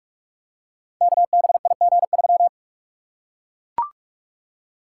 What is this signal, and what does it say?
Morse code, heard as a keyed tone, about 44 wpm, 700 Hz. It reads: KBIG3
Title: Code Groups - 5 Letter:Numbers - 45wpm